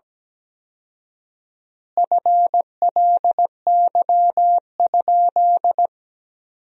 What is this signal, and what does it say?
Morse code, heard as a keyed tone, about 17 wpm, 700 Hz. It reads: FLY?
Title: 4x Word-Spacing - Sentences from Top 500 Words - 17wpm